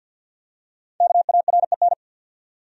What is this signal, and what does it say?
Morse code, heard as a keyed tone, about 53 wpm, 700 Hz. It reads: KAREN
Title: USA Common Names - 50wpm